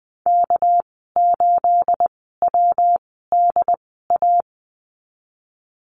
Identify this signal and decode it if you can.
Morse code, heard as a keyed tone, about 20 wpm, spaced as for 15 wpm, 700 Hz. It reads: K8WDA
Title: 231 Lesson 37g - Letter x - Call Signs 20wpm